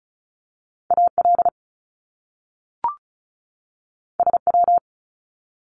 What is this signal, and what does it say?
Morse code, heard as a keyed tone, about 36 wpm, 700 Hz. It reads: AL SW
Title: Top Two Letter Combinations in Words 35wpm